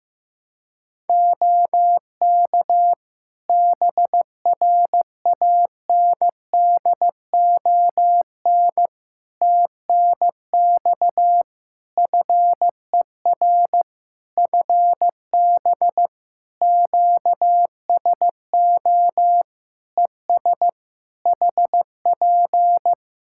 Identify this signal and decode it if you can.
Morse code, heard as a keyed tone, about 15 wpm, 700 Hz. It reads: OK BRANDON TNX FER FB QSO ES HP
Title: K1USN SST QSO Practice 15wpm